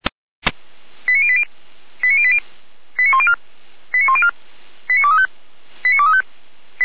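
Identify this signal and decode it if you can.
Five-tone selective call (ZVEI), heard as quick runs of five tones, tones of about 70 ms. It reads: 89080 89080 89104 89104 89235 89235
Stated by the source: ZVEI five-tone call received by SDR.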